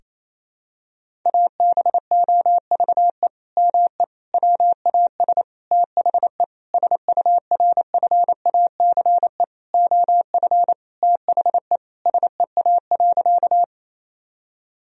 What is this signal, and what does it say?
Morse code, heard as a keyed tone, about 28 wpm, 700 Hz. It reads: ABOVE ME WAS THE SURFACE OF THE SEA.
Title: Sentences from Top 1000 Words - Review Part 3 - 28wpm